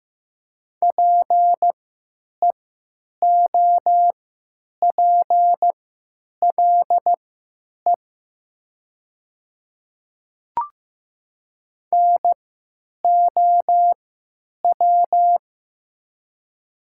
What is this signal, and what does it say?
Morse code, heard as a keyed tone, about 15 wpm, spaced as for 9 wpm, 700 Hz. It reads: PEOPLE NOW
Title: 3x Character Spacing - Top 100 Words - 15wpm